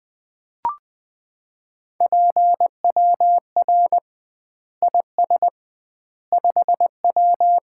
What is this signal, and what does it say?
Morse code, heard as a keyed tone, about 20 wpm, 700 Hz. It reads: PWR IS 5W
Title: Common QSO Phrases - 2x Word-Spacing - 20wpm